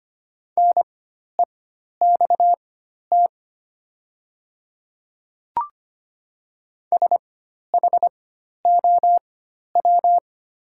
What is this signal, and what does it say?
Morse code, heard as a keyed tone, about 25 wpm, spaced as for 12 wpm, 690 Hz. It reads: NEXT SHOW
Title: 4x Character Spacing - Top 300 Words - 25wpm